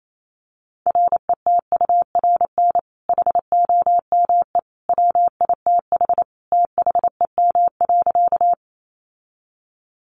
Morse code, heard as a keyed tone, about 28 wpm, 700 Hz. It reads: RETURN HOME WITH THEM.
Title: Sentences from Top 800 Words 28wpm